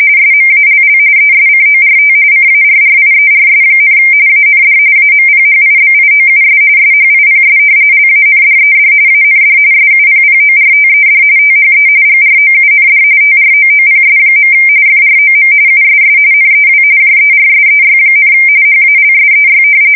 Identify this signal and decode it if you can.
RTTY, about 45.45 baud. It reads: SMALL ESCORT CARRIERS, AND AMPHIBIOUS SHIPS LANDED AND SUPPORTED THE GROUND TROOPS. FROM USS NEW JERSEY, HALSEY
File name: fsk170 / battleshipnj